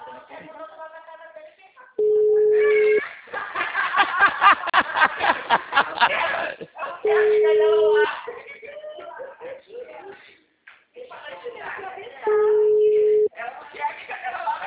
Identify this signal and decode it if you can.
Ringback tone (Europe, 425 Hz), with speech.